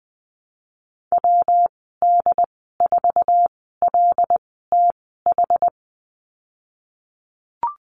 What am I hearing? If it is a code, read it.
Morse code, heard as a keyed tone, about 19 wpm, spaced as for 14 wpm, 700 Hz. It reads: WD4LTH
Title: 147 Lesson 24g - Letter y - Call Signs 20wpm